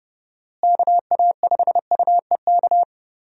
Morse code, heard as a keyed tone, about 32 wpm, 700 Hz. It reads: KA5UEK
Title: USA Call Signs - 6 Characters - Repeated 3x - 30wpm